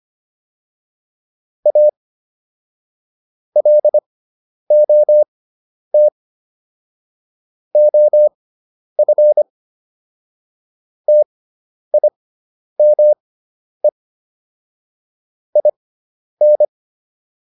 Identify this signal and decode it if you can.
Morse code, heard as a keyed tone, about 25 wpm, spaced as for 10 wpm, 600 Hz. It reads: A LOT OF TIME IN